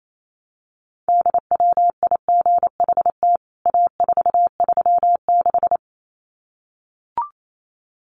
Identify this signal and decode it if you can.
Morse code, heard as a keyed tone, about 28 wpm, 700 Hz. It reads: DWIGHT A436
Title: CWT Exchange 28wpm